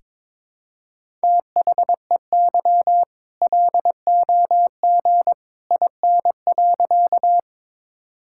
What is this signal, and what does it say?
Morse code, heard as a keyed tone, about 22 wpm, 690 Hz. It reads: THEY LOG IN.
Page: morse-code-ninja-practice-22wpm